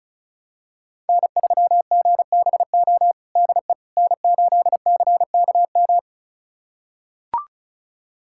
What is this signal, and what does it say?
Morse code, heard as a keyed tone, about 36 wpm, 700 Hz. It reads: N3GBO DE N8CKM
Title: USA Callsign DE Callsign - 5 Characters - 35wpm